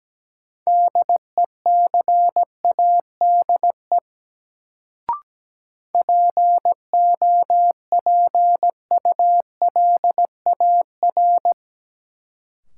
Morse code, heard as a keyed tone, about 17 wpm, 700 Hz. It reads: DECADE POPULAR
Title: Sets of 5 Words - New Word Drill - 17wpm